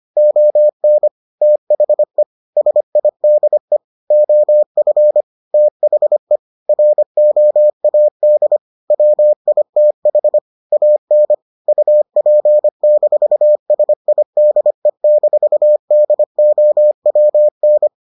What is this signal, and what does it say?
Morse code, heard as a keyed tone, about 25 wpm, 600 Hz. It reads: ON THE SIDE OF THE ROAD WITH AN UP-SIDE-DOWN